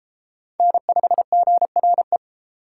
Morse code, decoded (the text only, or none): N5GRE